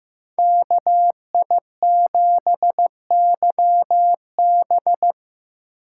Morse code, heard as a keyed tone, about 18 wpm, 700 Hz. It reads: KI7YB
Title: Speed Racing - USA Call Signs 15wpm